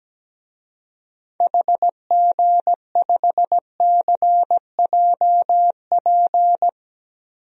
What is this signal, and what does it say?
Morse code, heard as a keyed tone, about 17 wpm, 700 Hz. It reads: HG5CJP
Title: DXCC-100 Call Signs - 6 Characters - 17wpm